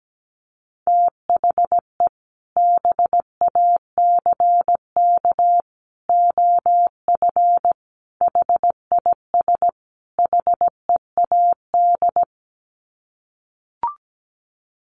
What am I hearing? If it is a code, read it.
Morse code, heard as a keyed tone, about 17 wpm, 700 Hz. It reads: THE BACK OF HIS HEAD
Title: Sets of 5 Words Encore 17wpm